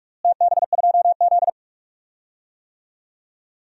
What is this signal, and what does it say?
Morse code, heard as a keyed tone, about 42 wpm, 700 Hz. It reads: TB2Z
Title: DXCC-100 Call Signs - 4 Characters - Repeated 2x - 45wpm